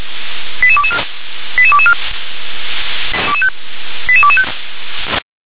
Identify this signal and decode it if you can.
Five-tone selective call (ZVEI), heard as quick runs of five tones, tones of about 70 ms. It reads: 88224 88224 88225
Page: ZVEI